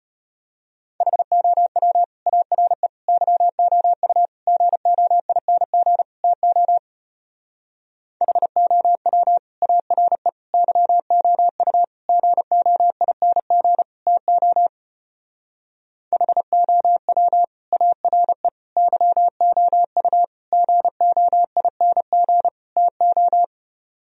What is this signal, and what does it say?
Morse code, heard as a keyed tone, about 33 wpm, 700 Hz. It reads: HOW ARE YOU GOING TO HOW ARE YOU GOING TO HOW ARE YOU GOING TO